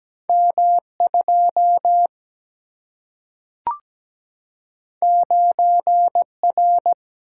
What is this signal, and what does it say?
Morse code, heard as a keyed tone, about 17 wpm, 700 Hz. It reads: M2 9R
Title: Single-Letter Single-Number 17wpm